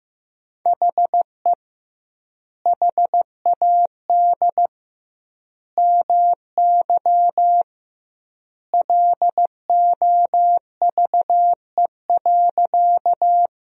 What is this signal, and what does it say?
Morse code, heard as a keyed tone, about 15 wpm, 700 Hz. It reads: HE HAD MY LOVE.
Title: 2x Word-Spacing - Sentences from Top 400 Words - 15wpm